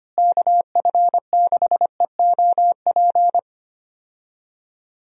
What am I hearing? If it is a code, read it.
Morse code, heard as a keyed tone, about 25 wpm, 700 Hz. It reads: KF6EOP
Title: USA Call Signs - 6 Characters - Repeated 3x - 25wpm